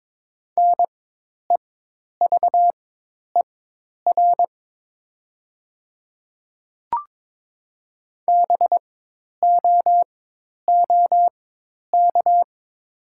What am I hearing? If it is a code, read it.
Morse code, heard as a keyed tone, about 22 wpm, spaced as for 10 wpm, 700 Hz. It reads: NEVER BOOK
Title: 4x Character Spacing - Top 500 Words - 22wpm